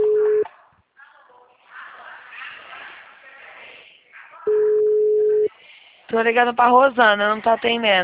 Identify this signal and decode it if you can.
Ringback tone (Europe, 425 Hz), with speech.